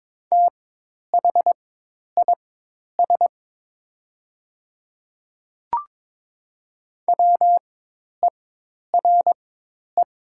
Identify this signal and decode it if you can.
Morse code, heard as a keyed tone, about 22 wpm, spaced as for 10 wpm, 700 Hz. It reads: THIS WERE
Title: 4x Character Spacing - Top 100 Words - 22wpm